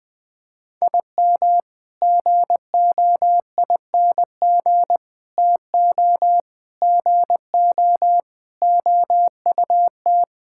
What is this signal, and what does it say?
Morse code, heard as a keyed tone, about 21 wpm, 700 Hz. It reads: IM GOING TO GO OUT
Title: Sentences with 8-Word Phrases - 20wpm